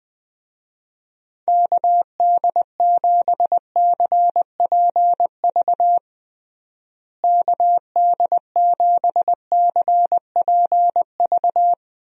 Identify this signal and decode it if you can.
Morse code, heard as a keyed tone, about 21 wpm, 700 Hz. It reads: KD7CPV KD7CPV
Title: USA Call Signs - 6 Characters - Repeated 2x - 20wpm